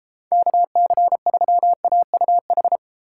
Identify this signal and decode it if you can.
Morse code, heard as a keyed tone, about 32 wpm, 700 Hz. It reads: KC3AUH